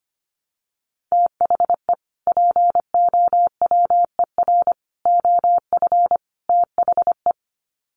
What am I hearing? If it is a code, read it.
Morse code, heard as a keyed tone, about 25 wpm, 700 Hz. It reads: THE POWER OF THE
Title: Sets of 4 Words 25wpm